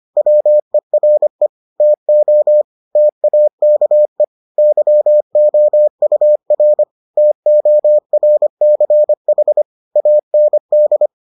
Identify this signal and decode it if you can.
Morse code, heard as a keyed tone, about 25 wpm, 600 Hz. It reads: WERE TO TAKE YOUR TORCH AND